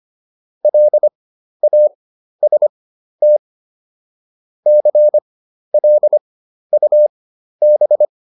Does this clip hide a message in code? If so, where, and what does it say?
Yes: Morse code from 0.6 to 8.1 s, heard as a keyed tone, about 25 wpm, spaced as for 12 wpm, 600 Hz. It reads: LAST CLUB